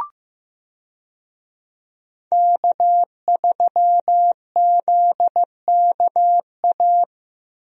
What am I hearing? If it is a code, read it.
Morse code, heard as a keyed tone, about 15 wpm, 700 Hz. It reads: K3ZKA
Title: USA Call Signs - 5 Characters - Repeated 2x - 15wpm